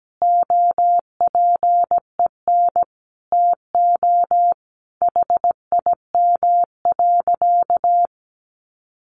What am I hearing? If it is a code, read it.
Morse code, heard as a keyed tone, about 17 wpm, 700 Hz. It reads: OPEN TO HIM.
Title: Sentences from Top 1200 Words 17wpm